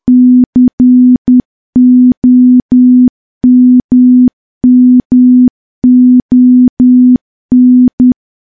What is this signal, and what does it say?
Morse code, heard as a keyed tone, about 10 wpm, 250 Hz. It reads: COMMON
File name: morse / morse-training-10WPM